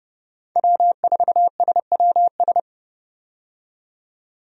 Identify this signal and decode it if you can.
Morse code, heard as a keyed tone, about 29 wpm, 700 Hz. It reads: W4SWS
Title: USA Call Signs - 5 Characters - 30wpm